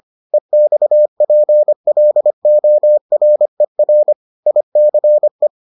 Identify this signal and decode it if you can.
Morse code, heard as a keyed tone, about 25 wpm, 600 Hz. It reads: EXPLORER ICE